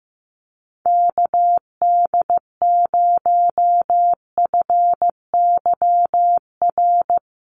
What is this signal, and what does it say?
Morse code, heard as a keyed tone, about 14 wpm, 700 Hz. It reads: KD0FYR
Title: USA Call Signs Repeated 2x 15wpm